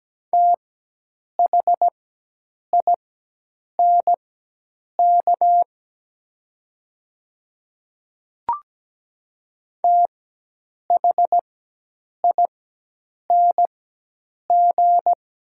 Morse code, heard as a keyed tone, about 17 wpm, spaced as for 8 wpm, 700 Hz. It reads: THINK THING
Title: 4x Character Spacing - Top 300 Words - 17wpm